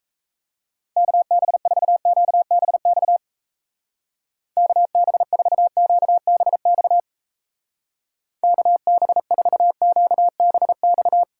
Morse code, heard as a keyed tone, about 37 wpm, 700 Hz. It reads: KB4QBX KB4QBX KB4QBX